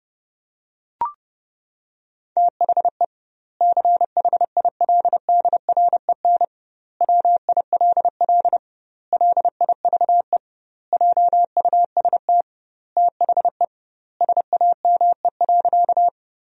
Morse code, heard as a keyed tone, about 32 wpm, 700 Hz. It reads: THE CHILDREN WILL LIVE JUST THE SAME.
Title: 2x Word-Spacing - Sentences from Top 300 Words - 30wpm